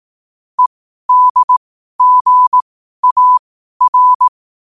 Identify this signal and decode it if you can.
Morse code, heard as a keyed tone, about 18 wpm, spaced as for 12 wpm, 1000 Hz. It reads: EDGAR